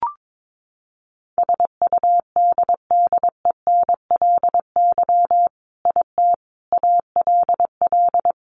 Morse code, heard as a keyed tone, about 22 wpm, 700 Hz. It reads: SUDDENLY IT ALL
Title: A Sound Of Thunder 22wpm